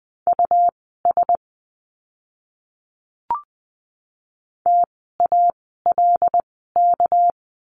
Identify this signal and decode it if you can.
Morse code, heard as a keyed tone, about 20 wpm, spaced as for 14 wpm, 700 Hz. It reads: US TALK
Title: 206 Lesson 33f - Letter j - Words 20wpm